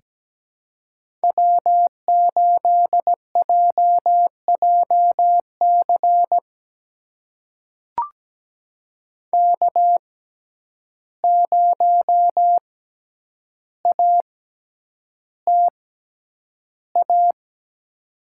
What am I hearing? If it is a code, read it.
Morse code, heard as a keyed tone, about 17 wpm, spaced as for 9 wpm, 700 Hz. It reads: W8JJC K0ATA
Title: USA Call Signs - 5 Characters - 6x Character Spacing - 17wpm